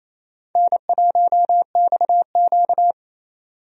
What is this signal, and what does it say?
Morse code, heard as a keyed tone, about 28 wpm, 700 Hz. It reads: N1XQ
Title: USA Call Signs - 4 Characters - 28wpm